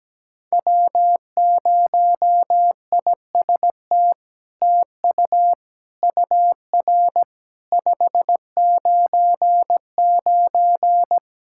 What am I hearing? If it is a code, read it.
Morse code, heard as a keyed tone, about 17 wpm, 700 Hz. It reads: W0IST TU UR 599
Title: SOTA QSO Practice 17wpm